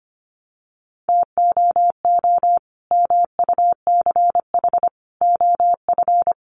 Morse code, heard as a keyed tone, about 25 wpm, 700 Hz. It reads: TOO MUCH OF
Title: Sentences from Top 200 Words 25wpm